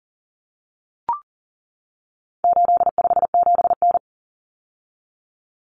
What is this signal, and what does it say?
Morse code, heard as a keyed tone, about 40 wpm, 700 Hz. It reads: 857N
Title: Common Cut Numbers - 4-Digit 40wpm